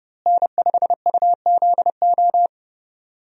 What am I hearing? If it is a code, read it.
Morse code, heard as a keyed tone, about 32 wpm, 700 Hz. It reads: N5UZO
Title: USA Call Signs - 5 Characters - Repeated 2x - 30wpm